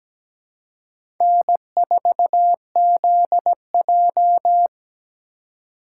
Morse code, heard as a keyed tone, about 17 wpm, 700 Hz. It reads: N4ZJ
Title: USA Call Signs - 4 Characters - 4x Character Spacing - 17wpm